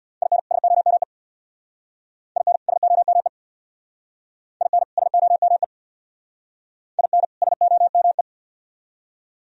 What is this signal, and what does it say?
Morse code, heard as a keyed tone, about 66 wpm, 700 Hz. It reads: IN SOME IN SOME IN SOME IN SOME